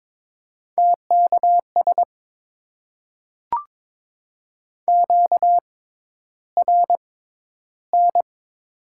Morse code, heard as a keyed tone, about 22 wpm, spaced as for 11 wpm, 700 Hz. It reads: TKS QRN
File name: Top 90 QSO Elements - 6x Character Spacing - 22wpm